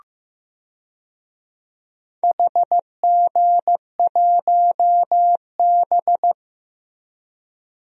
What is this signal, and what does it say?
Morse code, heard as a keyed tone, about 15 wpm, 700 Hz. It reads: HG1B
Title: DXCC-100 Call Signs - 4 Characters - Repeated 3x - 15wpm